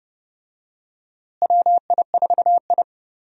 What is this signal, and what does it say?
Morse code, heard as a keyed tone, about 30 wpm, 700 Hz. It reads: WI4I
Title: USA Call Signs - 4 Characters - Repeated 3x - 30wpm